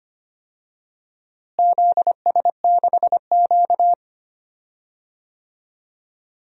Morse code, heard as a keyed tone, about 25 wpm, 700 Hz. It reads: ZS6Q
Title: DXCC-100 Call Signs - 4 Characters - Repeated 2x - 25wpm